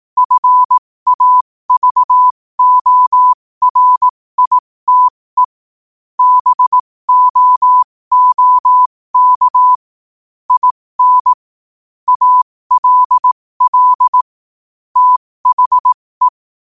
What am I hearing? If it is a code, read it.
Morse code, heard as a keyed tone, about 18 wpm, spaced as for 15 wpm, 1000 Hz. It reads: FAVORITE BOOK IN ALL THE
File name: qotdfiles / 15wpm